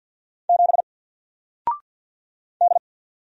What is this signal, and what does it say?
Morse code, heard as a keyed tone, about 48 wpm, 700 Hz. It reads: 7 D
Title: Single Character (Letter:Number:Punctuation:Prosign) 50wpm